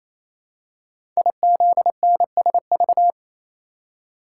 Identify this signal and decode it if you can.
Morse code, heard as a keyed tone, about 29 wpm, 700 Hz. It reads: IZNSV
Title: Code Groups - 5 Letters - 28wpm